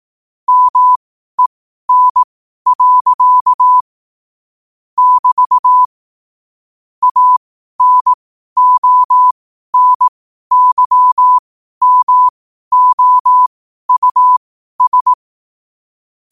Morse code, heard as a keyed tone, about 18 wpm, spaced as for 12 wpm, 1000 Hz. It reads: MEN. <BT> ANONYMOUS
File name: qotdfiles / 12wpm